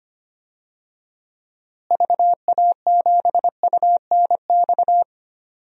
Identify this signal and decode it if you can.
Morse code, heard as a keyed tone, about 25 wpm, 700 Hz. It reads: VA7UNX
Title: DXCC-100 Call Signs - 6 Characters - Repeated 2x - 25wpm